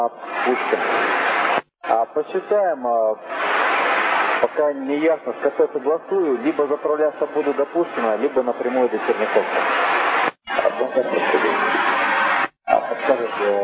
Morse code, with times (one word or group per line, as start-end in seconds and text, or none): none